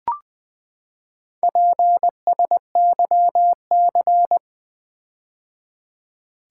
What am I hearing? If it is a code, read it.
Morse code, heard as a keyed tone, about 20 wpm, 700 Hz. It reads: PSYC
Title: Build A Word - 4 Letters At A Time - 12-Letter Words - 20wpm